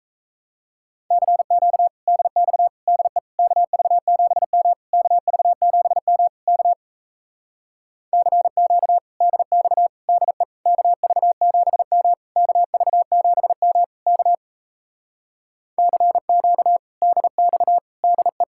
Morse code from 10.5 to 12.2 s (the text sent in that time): KV7M